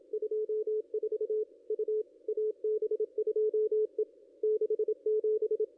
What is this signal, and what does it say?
Morse code, heard as a keyed tone, about 27 wpm, 410 Hz. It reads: 24 U AB2E 67
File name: contests / ss-cw-k5tr-op-n5rz -1900